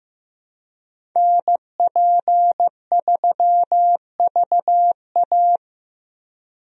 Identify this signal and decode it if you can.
Morse code, heard as a keyed tone, about 15 wpm, 700 Hz. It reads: NP3VA
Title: USA Call Signs - 5 Characters - Repeated 2x - 15wpm